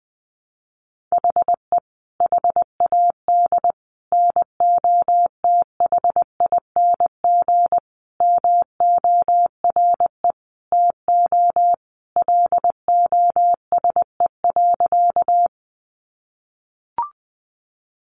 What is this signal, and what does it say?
Morse code, heard as a keyed tone, about 20 wpm, 700 Hz. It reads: HE HAD NOTHING MORE TO LOSE.
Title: Sentences from Top 1100 Words 20wpm